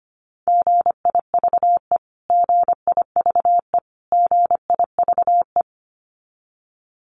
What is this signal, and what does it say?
Morse code, heard as a keyed tone, about 25 wpm, 700 Hz. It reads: GIVE GIVE GIVE
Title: Top 500 Words Repeated 25wpm